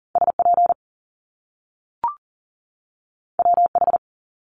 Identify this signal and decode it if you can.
Morse code, heard as a keyed tone, about 40 wpm, 700 Hz. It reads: SP WH
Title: Top Two Letter Combinations in Words 40wpm